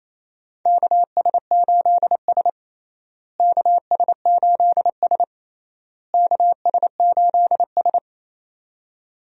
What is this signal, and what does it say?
Morse code, heard as a keyed tone, about 27 wpm, 700 Hz. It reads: KS8S KS8S KS8S
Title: USA Call Signs - 4 Characters - Repeated 3x - 28wpm